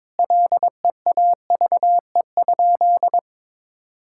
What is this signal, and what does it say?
Morse code, heard as a keyed tone, about 22 wpm, 700 Hz. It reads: LEAVE?
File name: practice / 22wpm / Sentences with 4-Word Phrases - 22wpm